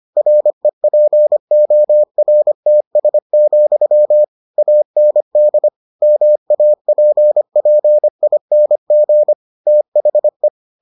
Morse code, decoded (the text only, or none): REPORTS, AND MAPPING THE